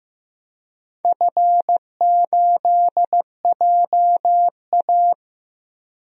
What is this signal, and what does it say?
Morse code, heard as a keyed tone, about 15 wpm, 700 Hz. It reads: F8JA
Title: DXCC-100 Call Signs - 4 Characters - 15wpm